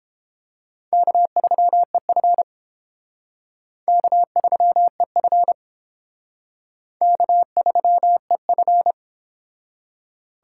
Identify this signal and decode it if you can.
Morse code, heard as a keyed tone, about 30 wpm, 700 Hz. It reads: K3EF K3EF K3EF